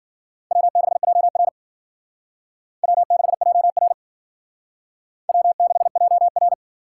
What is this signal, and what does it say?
Morse code, heard as a keyed tone, about 52 wpm, 700 Hz. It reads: W6JR W6JR W6JR